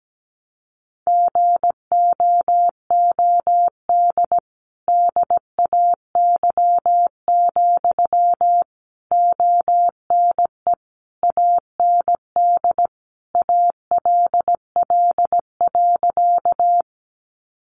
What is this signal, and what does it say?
Morse code, heard as a keyed tone, about 17 wpm, 700 Hz. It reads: GOOD DAY, ONE AND ALL.
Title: Sentences from Top 200 Words 17wpm